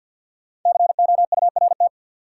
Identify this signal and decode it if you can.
Morse code, heard as a keyed tone, about 53 wpm, 700 Hz. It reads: COURT
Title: Sentences from Top 1600 Words - New Word Drill Alt - 50wpm